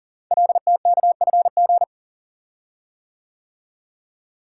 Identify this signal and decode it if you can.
Morse code, heard as a keyed tone, about 44 wpm, 700 Hz. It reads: LTKFG
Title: Code Groups - 5 Letters - 40wpm